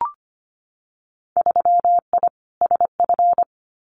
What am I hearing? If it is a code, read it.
Morse code, heard as a keyed tone, about 25 wpm, 700 Hz. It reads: 3I SF
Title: Winter Field Day Exchanges 25wpm